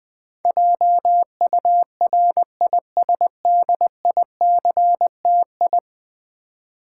Morse code, heard as a keyed tone, about 20 wpm, 700 Hz. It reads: JURISDICTI
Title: Build A Word - 5 Letters At A Time - 12-Letter Words - 20wpm